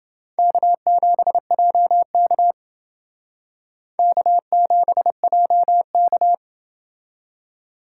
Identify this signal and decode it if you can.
Morse code, heard as a keyed tone, about 28 wpm, 700 Hz. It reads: K7JK K7JK